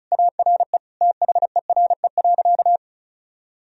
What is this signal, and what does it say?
Morse code, heard as a keyed tone, about 36 wpm, 700 Hz. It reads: ARE THERE.
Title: Sentences from Top 1000 Words - Review Part 1 - 35wpm